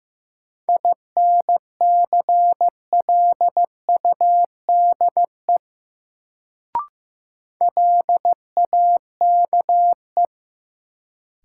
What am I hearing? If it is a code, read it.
Morse code, heard as a keyed tone, about 15 wpm, 700 Hz. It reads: INCLUDE LAKE
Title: Sentences from Top 1000 Words - Review Part 3 - New Word Drill - 15wpm